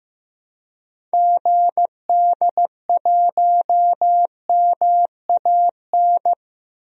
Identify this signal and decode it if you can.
Morse code, heard as a keyed tone, about 15 wpm, 700 Hz. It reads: GD1MAN
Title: DXCC-100 Call Signs - 6 Characters - Repeated 3x - 15wpm